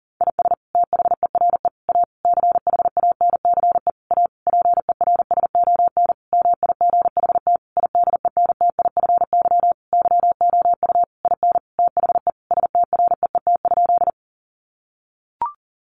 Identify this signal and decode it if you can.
Morse code, heard as a keyed tone, about 37 wpm, 700 Hz. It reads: IS THERE A CHANCE A PERSON MIGHT IDENTIFY YOU IN THE STREET?
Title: Sentences from Top 900 Words 40wpm